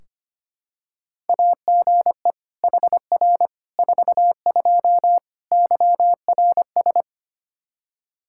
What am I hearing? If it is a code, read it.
Morse code, heard as a keyed tone, about 25 wpm, 700 Hz. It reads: AGE HR 42 YRS
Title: Practice QSO - Exchange 3 - More Info - 25wpm